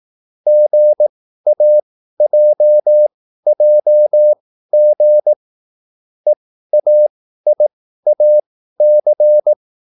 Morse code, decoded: GAJJG EAIAC

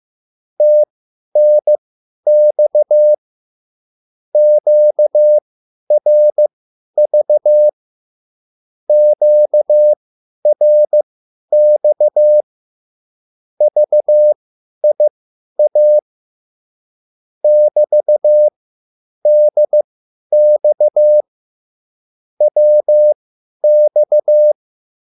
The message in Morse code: TNX QRV QRX VIA <BT>DX WX

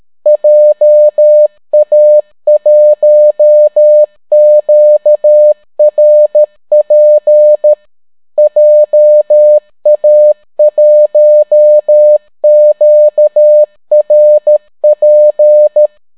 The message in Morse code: JA1QRP JA1QRP

